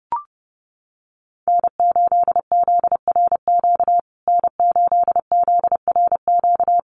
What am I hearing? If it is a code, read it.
Morse code, heard as a keyed tone, about 30 wpm, 700 Hz. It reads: N8ZRQ N8ZRQ
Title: USA Call Signs Repeated 3x 30wpm